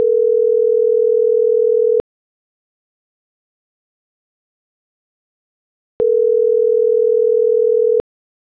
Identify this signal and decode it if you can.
Ringback tone (North America), 440 + 480 Hz.